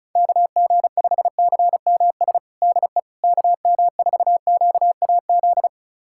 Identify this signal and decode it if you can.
Morse code, heard as a keyed tone, about 34 wpm, 700 Hz. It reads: KG5CMS DE KM4QAZ
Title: USA Callsign DE Callsign - 6 Characters - 35wpm